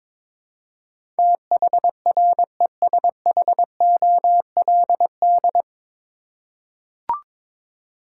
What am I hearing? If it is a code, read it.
Morse code, heard as a keyed tone, about 21 wpm, 700 Hz. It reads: THRESHOLD
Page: Morse Code Ninja practice 22wpm